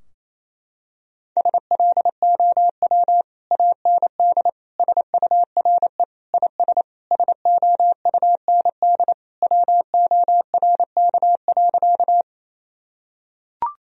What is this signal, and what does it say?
Morse code, heard as a keyed tone, about 28 wpm, 700 Hz. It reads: SLOW AND SURE IS SOUND WORK.
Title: Sentences from Top 1000 Words - Review Part 2 - 28wpm